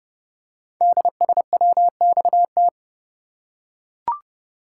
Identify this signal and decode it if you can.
Morse code, heard as a keyed tone, about 30 wpm, 700 Hz. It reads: DSWXT
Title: Code Groups - 5 Letters - 30wpm